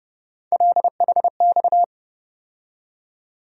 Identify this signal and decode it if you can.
Morse code, heard as a keyed tone, about 30 wpm, 700 Hz. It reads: LHX
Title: USA Call Sign Postfixes - 3 Letters 30wpm